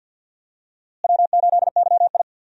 Morse code, heard as a keyed tone, about 49 wpm, 700 Hz. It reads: W8YI
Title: USA Call Signs - 4 Characters - Repeated 3x - 50wpm